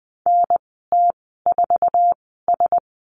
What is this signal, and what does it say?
Morse code, heard as a keyed tone, about 20 wpm, spaced as for 14 wpm, 700 Hz. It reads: NT4S
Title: 062 Lesson 11g - Letter h - Call Signs 20wpm